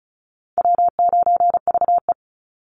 Morse code, heard as a keyed tone, about 34 wpm, 700 Hz. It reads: W9VE
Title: Call Signs - 35wpm